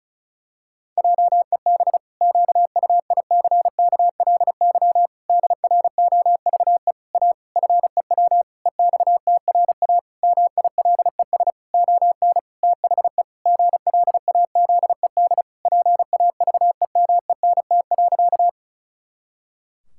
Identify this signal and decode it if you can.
Morse code, heard as a keyed tone, about 34 wpm, 700 Hz. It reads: JEB QUICKLY DROVE A FEW EXTRA MILES ON THE GLAZED PAVEMENT.
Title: Pangram Challenge v2 - 35wpm